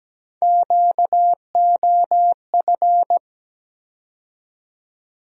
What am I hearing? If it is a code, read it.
Morse code, heard as a keyed tone, about 17 wpm, 700 Hz. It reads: QOF
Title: USA Call Sign Postfixes - 3 Letters 17wpm